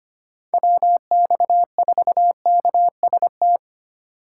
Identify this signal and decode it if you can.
Morse code, heard as a keyed tone, about 29 wpm, 700 Hz. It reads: WX4KST
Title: Speed Racing - USA Call Signs 25wpm